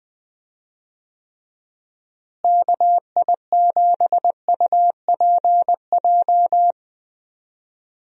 Morse code, heard as a keyed tone, about 20 wpm, 700 Hz. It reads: KI7UPJ
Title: USA Call Signs - 6 Characters - Repeated 2x - 20wpm